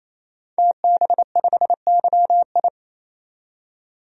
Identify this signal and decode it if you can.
Morse code, heard as a keyed tone, about 28 wpm, 700 Hz. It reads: TB5YI
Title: DXCC-100 Call Signs - 5 Characters - 28wpm